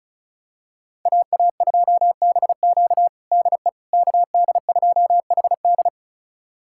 Morse code, heard as a keyed tone, about 36 wpm, 700 Hz. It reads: AA2BQ DE KD2HD
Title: USA Callsign DE Callsign - 5 Characters - 35wpm